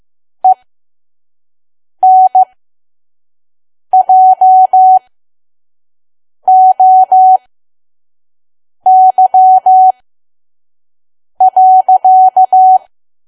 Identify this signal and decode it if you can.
Morse code, heard as a keyed tone, about 15 wpm, spaced as for 5 wpm, 750 Hz. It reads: ENJOY.
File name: Archive / 5 WPM files / PM